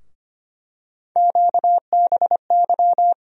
Morse code, heard as a keyed tone, about 25 wpm, 700 Hz. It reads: QBY
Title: USA Call Sign Postfixes - 3 Letters 25wpm